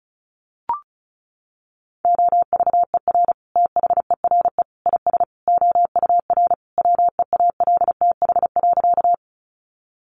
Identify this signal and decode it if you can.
Morse code, heard as a keyed tone, about 34 wpm, 700 Hz. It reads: OVER THERE IS OUR WEALTH.
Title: Sentences from Top 1200 Words 35wpm